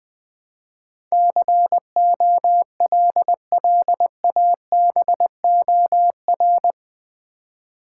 Morse code, heard as a keyed tone, about 20 wpm, 700 Hz. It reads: COLLABOR
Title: Build A Word - 4 Letters At A Time - 11-Letter Words - 20wpm